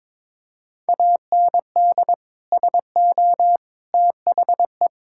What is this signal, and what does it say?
Morse code, heard as a keyed tone, about 22 wpm, 700 Hz. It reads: AND SO THE
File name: practice / 22wpm / Sentences with 2-Word Phrases - 22wpm